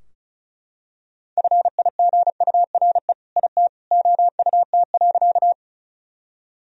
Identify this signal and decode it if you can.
Morse code, heard as a keyed tone, about 36 wpm, 700 Hz. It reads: FIGURE IT OUT.
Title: Sentences from Top 1000 Words - Review Part 2 - 35wpm